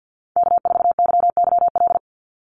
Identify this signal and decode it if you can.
Morse code, heard as a keyed tone, about 50 wpm, 700 Hz. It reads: K4YYL
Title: Call Signs Repeated 2x - 50wpm